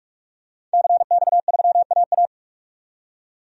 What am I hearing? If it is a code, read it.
Morse code, heard as a keyed tone, about 44 wpm, 700 Hz. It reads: CX3AA
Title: DXCC-100 Call Signs - 5 Characters - 45wpm